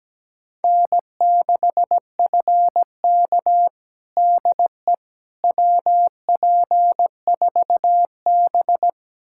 Morse code, heard as a keyed tone, about 17 wpm, 700 Hz. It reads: N6FK DE WP4B
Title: USA Callsign DE Callsign - 4 Characters - 17wpm